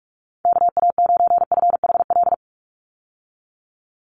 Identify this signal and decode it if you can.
Morse code, heard as a keyed tone, about 44 wpm, 700 Hz. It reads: KA9FHL
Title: USA Call Signs 45wpm